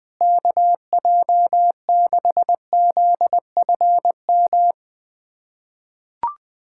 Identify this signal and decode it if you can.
Morse code, heard as a keyed tone, about 21 wpm, 700 Hz. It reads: KJ6ZFM
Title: USA Callsign DE Callsign 20wpm